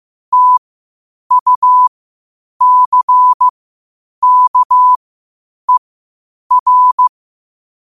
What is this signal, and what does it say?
Morse code, heard as a keyed tone, about 15 wpm, spaced as for 8 wpm, 1000 Hz. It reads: TUCKER
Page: New quotes every day in morse code at 8 Words per minute.